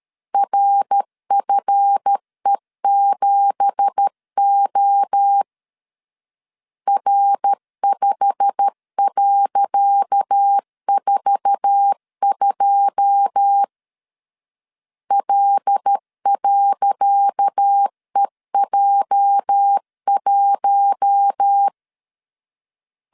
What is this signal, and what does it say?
Morse code, heard as a keyed tone, about 13 wpm, 800 Hz. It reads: RFE7O R5.42 L.EJ1
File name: LEZIONI / 10 SIGNS / 10C DOT 12wpm 2451